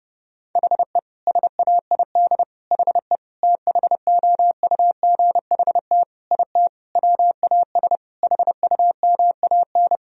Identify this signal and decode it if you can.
Morse code, heard as a keyed tone, about 32 wpm, 700 Hz. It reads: HE SAID HE THOUGHT IT WAS HUMAN
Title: Sentences from Top 1000 Words - Review Part 3 - 30wpm